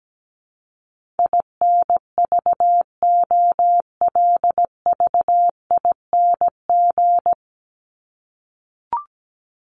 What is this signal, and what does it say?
Morse code, heard as a keyed tone, about 17 wpm, 700 Hz. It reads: INVOLVING
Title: ING Suffix - Top 100 Words with Base Words 17wpm